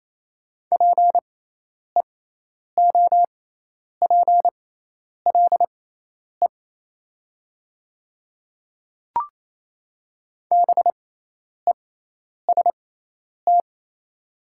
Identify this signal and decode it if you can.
Morse code, heard as a keyed tone, about 28 wpm, spaced as for 10 wpm, 700 Hz. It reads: PEOPLE BEST
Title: 6x Character Spacing - Top 500 Words - 28wpm